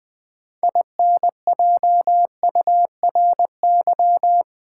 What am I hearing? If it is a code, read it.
Morse code, heard as a keyed tone, about 20 wpm, 700 Hz. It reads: INJURY
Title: Build A Word - 2 Letters At A Time - 6-Letter Words - 20wpm